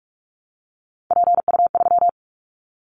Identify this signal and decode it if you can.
Morse code, heard as a keyed tone, about 46 wpm, 700 Hz. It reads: PU3
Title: Callsign Prefixes for Top 100 DXCC - 45wpm